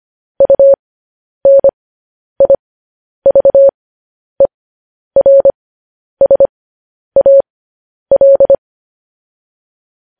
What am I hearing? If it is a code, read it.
Morse code, heard as a keyed tone, about 25 wpm, spaced as for 10 wpm, 550 Hz. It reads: UNIVERSAL
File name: podcast / News headlines-1x-25x10